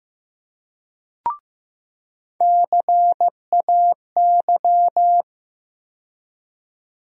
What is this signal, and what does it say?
Morse code, heard as a keyed tone, about 15 wpm, 700 Hz. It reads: CAY
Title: USA Call Sign Postfixes - 3 Letters 15wpm